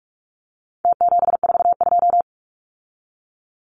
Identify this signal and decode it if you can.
Morse code, heard as a keyed tone, about 44 wpm, 700 Hz. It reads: T742
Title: Numbers - Contest Style - 4-Digit 45wpm